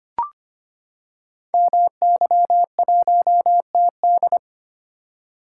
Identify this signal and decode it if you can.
Morse code, heard as a keyed tone, about 25 wpm, 700 Hz. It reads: MY1TD
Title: Code Groups - 5 Letter:Numbers - 25wpm